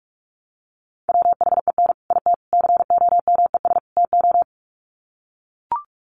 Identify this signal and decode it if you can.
Morse code, heard as a keyed tone, about 44 wpm, 700 Hz. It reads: WHEN IT COMES TO
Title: Sets of 4 Words 45wpm